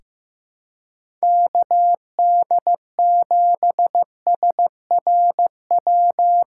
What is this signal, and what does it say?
Morse code, heard as a keyed tone, about 16 wpm, 700 Hz. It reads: KD7SRW
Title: USA Callsign DE Callsign - 6 Characters - 15wpm